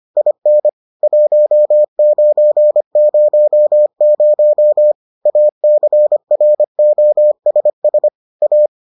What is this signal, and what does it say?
Morse code, heard as a keyed tone, about 25 wpm, 600 Hz. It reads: IN 1900 ACROSS A